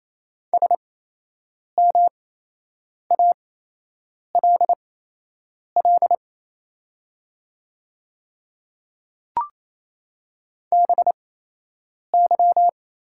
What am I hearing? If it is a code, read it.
Morse code, heard as a keyed tone, about 27 wpm, spaced as for 8 wpm, 690 Hz. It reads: SMALL BY
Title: 8x Character Spacing - Top 300 Words - 28wpm